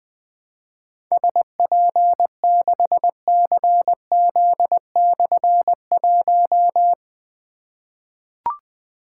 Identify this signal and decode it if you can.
Morse code, heard as a keyed tone, about 20 wpm, 700 Hz. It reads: SP6CZ/1
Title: Difficult Call Signs 20wpm